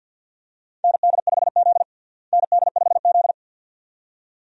Morse code, heard as a keyed tone, about 47 wpm, 700 Hz. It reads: ND5Z ND5Z
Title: USA Call Signs - 4 Characters - Repeated 2x - 50wpm